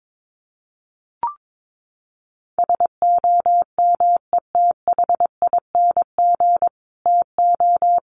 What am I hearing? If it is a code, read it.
Morse code, heard as a keyed tone, about 22 wpm, 700 Hz. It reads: SOMETHING TO
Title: Sets of 5 Words Encore 22wpm